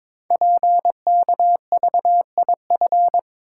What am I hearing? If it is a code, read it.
Morse code, heard as a keyed tone, about 22 wpm, 700 Hz. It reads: PKVIF
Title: Code Groups - 5 Letters - 22wpm